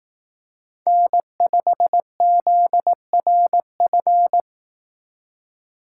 Morse code, heard as a keyed tone, about 18 wpm, 700 Hz. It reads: N5ZRF
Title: Speed Racing - USA 1x3 Call Signs 15wpm